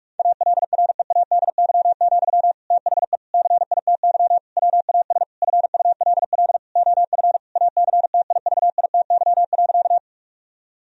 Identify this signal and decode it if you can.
Morse code, heard as a keyed tone, about 46 wpm, 700 Hz. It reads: ALREADY, THE CITY WAS FULL OF ACTIVITY.
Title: Sentences from Top 1000 Words - Review Part 4 - 45wpm